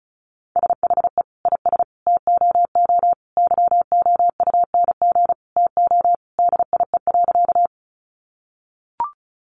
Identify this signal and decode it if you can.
Morse code, heard as a keyed tone, about 34 wpm, 700 Hz. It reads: SHE IS TOO YOUNG TO DIE.
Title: Sentences from Top 1300 Words 35wpm